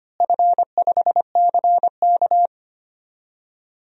Morse code, heard as a keyed tone, about 25 wpm, 700 Hz. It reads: F5CK
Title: DXCC-100 Call Signs - 4 Characters - Repeated 3x - 25wpm